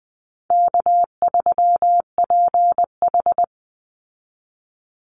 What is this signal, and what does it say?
Morse code, heard as a keyed tone, about 20 wpm, 700 Hz. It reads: K3PH
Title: Easy Call Signs Repeated 2x - 20wpm